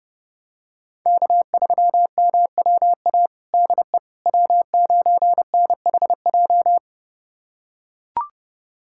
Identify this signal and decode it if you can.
Morse code, heard as a keyed tone, about 32 wpm, 700 Hz. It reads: K3MWA DE W9NHJ
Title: USA Callsign DE Callsign - 5 Characters - 30wpm